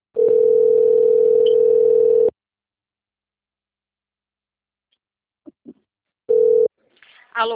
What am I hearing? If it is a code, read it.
Ringback tone (North America), 440 + 480 Hz.
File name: rg-1600-unknown- - - .403